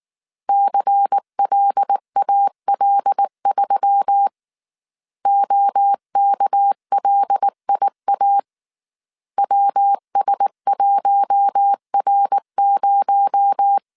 Morse code, heard as a keyed tone, about 19 wpm, 800 Hz. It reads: CLAL3 OKLIA WS1R0
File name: CW / DET / 1 6 1500C S A 18wpm 2255